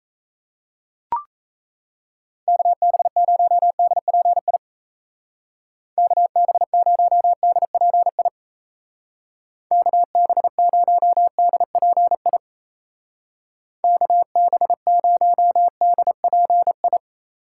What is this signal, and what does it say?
Morse code, heard as a keyed tone, about 36 wpm, 700 Hz. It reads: KB0DPI KB0DPI KB0DPI KB0DPI